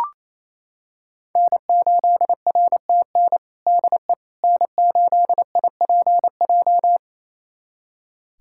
Morse code, heard as a keyed tone, about 28 wpm, 700 Hz. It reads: N8RTN DE N8IPJ
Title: USA Callsign DE Callsign - 5 Characters - 28wpm